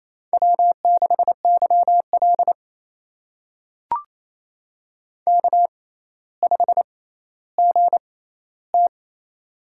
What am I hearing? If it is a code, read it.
Morse code, heard as a keyed tone, about 28 wpm, spaced as for 14 wpm, 700 Hz. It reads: W6YL K5GT
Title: USA Call Signs - 4 Characters - 6x Character Spacing - 28wpm